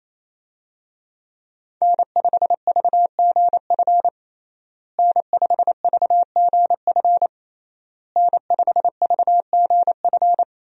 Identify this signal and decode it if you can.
Morse code, heard as a keyed tone, about 27 wpm, 700 Hz. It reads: N5VGF N5VGF N5VGF
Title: USA Call Signs - 5 Characters - Repeated 3x - 28wpm